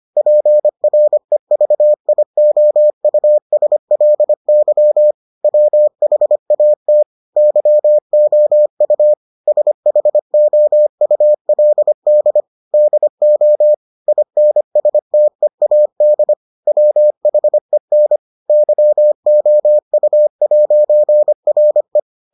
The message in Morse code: PREVIOUSLY WHAT YOU SHOULD DO INSTEAD WHEN YOU'RE

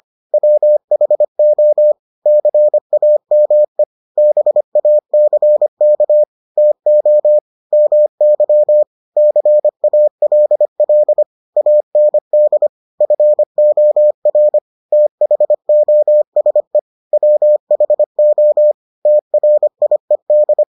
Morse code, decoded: WHO CAME BACK TO MY CALL AND FOR THOSE WHO TRIED

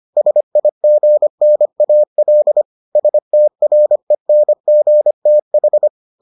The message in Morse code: SIGNAL STRENGTH

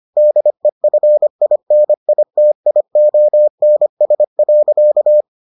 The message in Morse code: DEFINITIONS.

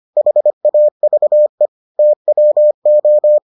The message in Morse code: HAVE TWO

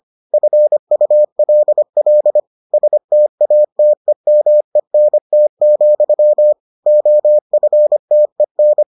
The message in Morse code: FULL STATEMENT, OFTEN

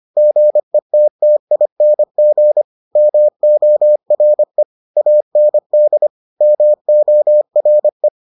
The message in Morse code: GETTING MORE AND MORE

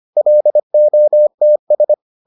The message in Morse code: LOTS

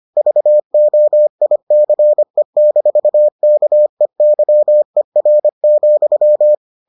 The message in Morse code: VOICE-KEYER,

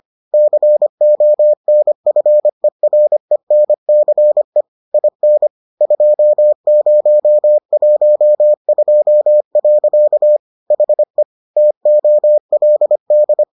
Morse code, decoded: CONFERENCE IN 2012. HE TOLD

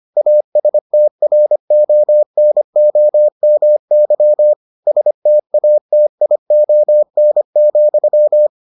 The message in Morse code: ASTRONOMY STATION,

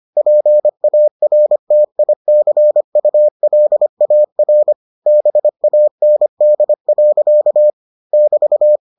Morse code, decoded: PARTICULAR BAND. <BT>

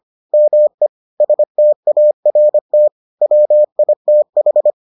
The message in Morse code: ME START WITH

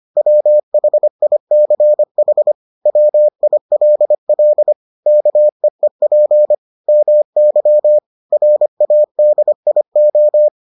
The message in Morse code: WHICH WILL KEEP MY RADIO

